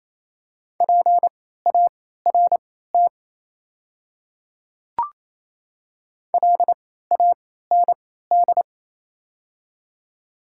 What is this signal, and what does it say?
Morse code, heard as a keyed tone, about 28 wpm, spaced as for 16 wpm, 700 Hz. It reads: PART LAND
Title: 3x Character Spacing - Top 200 Words - 28wpm